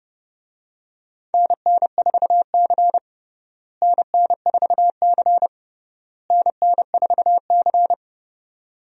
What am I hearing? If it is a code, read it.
Morse code, heard as a keyed tone, about 30 wpm, 700 Hz. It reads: NN4C NN4C NN4C
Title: USA Call Signs - 4 Characters - Repeated 3x - 30wpm